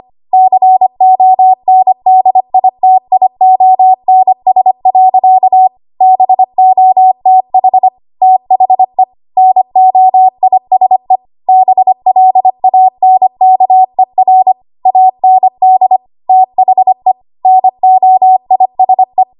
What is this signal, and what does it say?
Morse code, heard as a keyed tone, about 25 wpm, 750 Hz. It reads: CONDITIONS. BOTH THE NOISE BLANKER AND THE NOISE
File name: Archive / 25 WPM / 25WPM